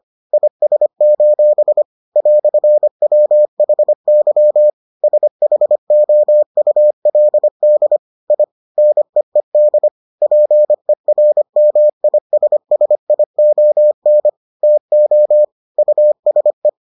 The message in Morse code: IS: "WHY SHOULD I NEED PERMISSION TO USE

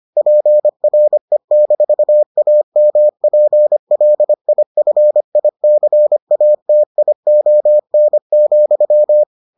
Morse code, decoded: PRE-AMPLIFICATION,

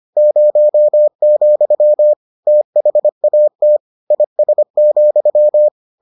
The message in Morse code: 0, THAT IS,

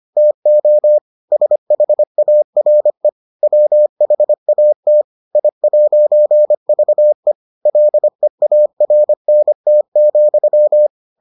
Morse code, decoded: TO SHARE WHAT I'VE LEARNT,

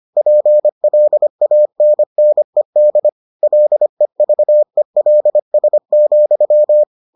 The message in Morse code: PLANNED LEVELS,

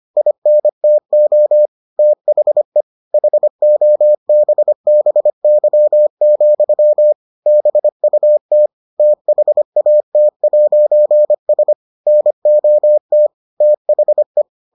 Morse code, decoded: INTO THE HOBBY, BUT THAT'S NOT THE

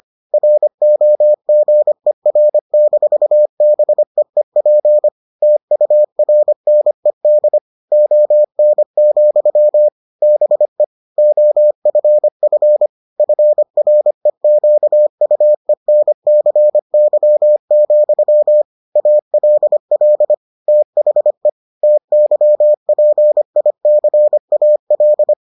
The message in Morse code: ROGER-BEEP TURNED ON, BE OFF FREQUENCY, ALL THE TYPICAL

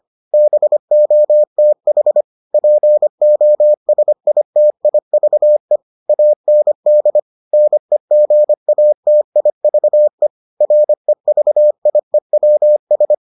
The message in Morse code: BOTH POSITIVE AND NEGATIVE REVIEWS

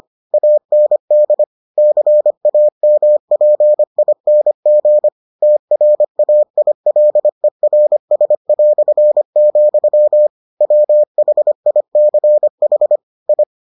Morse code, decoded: AND CAMPING TRAILERS", WHICH I

